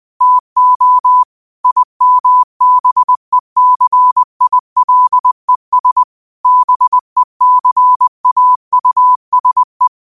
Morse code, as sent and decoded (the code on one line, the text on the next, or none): - --- / .. -- -... -.-. .. .-.. ... / -... -.-. .- ..- ... .
TO IMBECILES BECAUSE